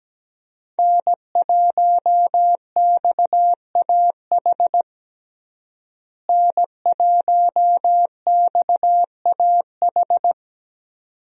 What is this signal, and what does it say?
Morse code, heard as a keyed tone, about 17 wpm, 700 Hz. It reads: N1XAH N1XAH
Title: USA Call Signs - 5 Characters - Repeated 2x - 17wpm